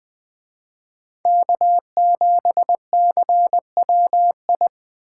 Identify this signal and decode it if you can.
Morse code, heard as a keyed tone, about 20 wpm, 700 Hz. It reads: K7CWI
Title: Speed Racing - USA 1x3 Call Signs 17wpm